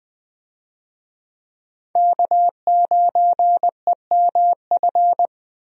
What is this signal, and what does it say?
Morse code, heard as a keyed tone, about 20 wpm, 700 Hz. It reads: K9EMF
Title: USA Call Signs - 5 Characters - Repeated 2x - 20wpm